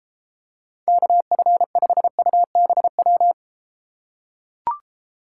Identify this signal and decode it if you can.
Morse code, heard as a keyed tone, about 33 wpm, 700 Hz. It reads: KF5UBW